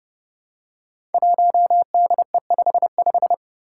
Morse code, heard as a keyed tone, about 29 wpm, 700 Hz. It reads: 1DE55
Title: Five Challenging Letter-Number Combinations 30wpm